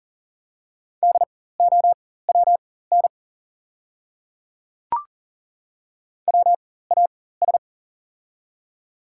Morse code, heard as a keyed tone, about 40 wpm, spaced as for 19 wpm, 700 Hz. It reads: DOWN WAS
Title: 4x Character Spacing - Top 500 Words - 40wpm